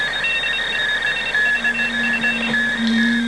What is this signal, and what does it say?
Morse code, heard as a keyed tone, about 25 wpm, 2700 Hz, with noise. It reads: DE SVF